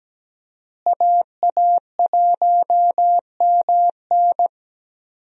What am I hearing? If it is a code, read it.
Morse code, heard as a keyed tone, about 17 wpm, 700 Hz. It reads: AA1MN
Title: USA Call Signs - 5 Characters - Repeated 3x - 17wpm